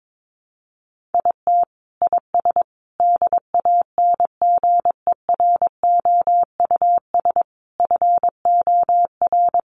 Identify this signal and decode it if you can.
Morse code, heard as a keyed tone, about 22 wpm, 700 Hz. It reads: IT IS DANGEROUS FOR
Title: Sentences from Top 1200 Words 22wpm